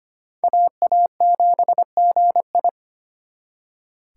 Morse code, heard as a keyed tone, about 35 wpm, spaced as for 30 wpm, 700 Hz. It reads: AA7GI
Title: Speed Racing - USA Call Signs 25wpm